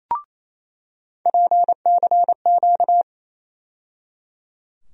Morse code, heard as a keyed tone, about 27 wpm, 700 Hz. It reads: PCQ
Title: USA Call Sign Postfixes - 3 Letters 28wpm